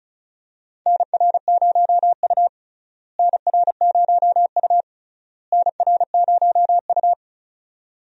Morse code, heard as a keyed tone, about 34 wpm, 700 Hz. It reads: NR0U NR0U NR0U
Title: USA Call Signs - 4 Characters - Repeated 3x - 35wpm